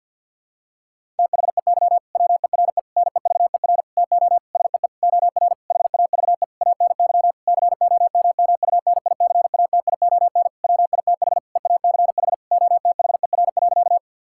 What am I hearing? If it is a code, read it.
Morse code, heard as a keyed tone, about 52 wpm, 700 Hz. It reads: THEY WERE NEVER TO SEE OR HAVE ANY COMMUNICATION WITH EACH OTHER.
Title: Sentences from Top 1000 Words - Review Part 6 - 50wpm